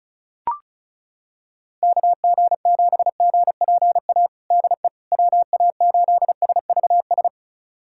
Morse code, heard as a keyed tone, about 36 wpm, 700 Hz. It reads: KG7GPA DE WA8SVS
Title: USA Callsign DE Callsign 35wpm